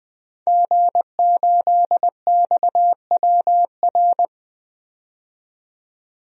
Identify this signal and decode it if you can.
Morse code, heard as a keyed tone, about 20 wpm, 700 Hz. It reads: G8XWR
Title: DXCC-100 Call Signs - 5 Characters - 20wpm